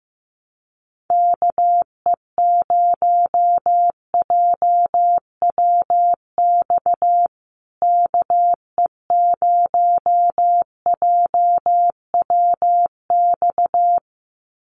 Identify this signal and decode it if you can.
Morse code, heard as a keyed tone, about 15 wpm, 700 Hz. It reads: KE0JWX KE0JWX
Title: USA Call Signs Repeated 2x 15wpm